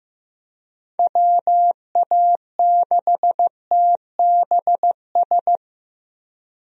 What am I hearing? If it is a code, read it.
Morse code, heard as a keyed tone, about 15 wpm, 700 Hz. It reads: WA6TBS
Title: USA Call Signs - 6 Characters - 15wpm